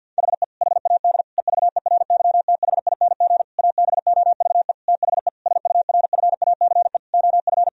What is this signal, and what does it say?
Morse code, heard as a keyed tone, about 50 wpm, 700 Hz. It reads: HE HAD EVERYTHING ABOVE THE SURFACE OF
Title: Sentences from Top 1000 Words - Review Part 3 - 50wpm